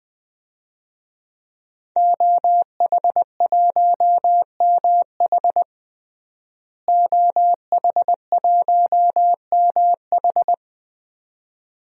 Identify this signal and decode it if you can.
Morse code, heard as a keyed tone, about 21 wpm, 700 Hz. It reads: OH1MH OH1MH
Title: DXCC-100 Call Signs - 5 Characters - Repeated 2x - 20wpm